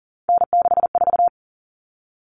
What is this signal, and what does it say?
Morse code, heard as a keyed tone, about 40 wpm, 700 Hz. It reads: N64
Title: Common Cut Numbers - 3-Digit 40wpm